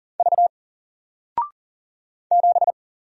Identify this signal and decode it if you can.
Morse code, heard as a keyed tone, about 40 wpm, 700 Hz. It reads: V 7
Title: Single Letter-Number Alt 40wpm